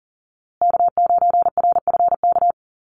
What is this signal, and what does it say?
Morse code, heard as a keyed tone, about 40 wpm, 700 Hz. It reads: K9RFK
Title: USA Call Signs Repeated 3x 40wpm